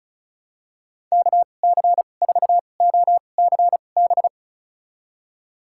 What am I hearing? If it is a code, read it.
Morse code, heard as a keyed tone, about 36 wpm, spaced as for 25 wpm, 700 Hz. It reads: KC4OCB
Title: USA Call Signs - 2x Character Spacing - 35wpm